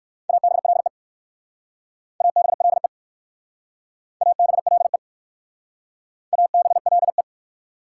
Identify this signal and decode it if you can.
Morse code, heard as a keyed tone, about 54 wpm, 690 Hz. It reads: ABLE ABLE ABLE ABLE
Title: Speed-Racing Prefixes and Suffixes 45wpm